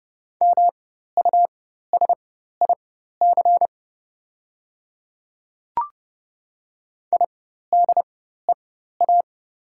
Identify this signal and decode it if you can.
Morse code, heard as a keyed tone, about 30 wpm, spaced as for 14 wpm, 700 Hz. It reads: MUSIC IDEA
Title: 4x Character Spacing - Top 500 Words - 30wpm